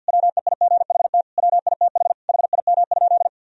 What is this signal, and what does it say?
Morse code, heard as a keyed tone, about 51 wpm, 700 Hz. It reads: WEIGHT WITH HIM?
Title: Sentences from Top 1000 Words - Review Part 3 - 50wpm